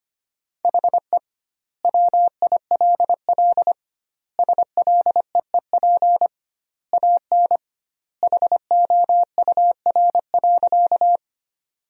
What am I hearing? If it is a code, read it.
Morse code, heard as a keyed tone, about 25 wpm, 700 Hz. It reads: HE WILL SLEEP AN HOUR.
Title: 2x Word-Spacing - Sentences from Top 500 Words - 25wpm